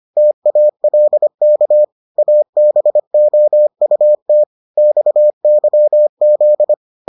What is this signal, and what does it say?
Morse code, heard as a keyed tone, about 25 wpm, 600 Hz. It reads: TALK ABOUT XYZ